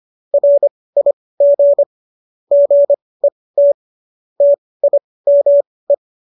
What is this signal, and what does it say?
Morse code, heard as a keyed tone, about 25 wpm, spaced as for 18 wpm, 570 Hz. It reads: RIG GET TIME